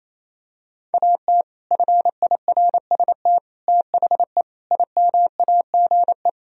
Morse code, heard as a keyed tone, about 28 wpm, 700 Hz. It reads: AT FIRST THE IMAGE
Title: Sentences from Top 1000 Words - Review Part 4 - 28wpm